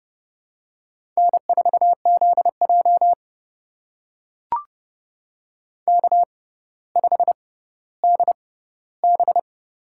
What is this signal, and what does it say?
Morse code, heard as a keyed tone, about 32 wpm, spaced as for 16 wpm, 700 Hz. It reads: N4ZJ K5DB
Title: USA Call Signs - 6x Character Spacing - 30wpm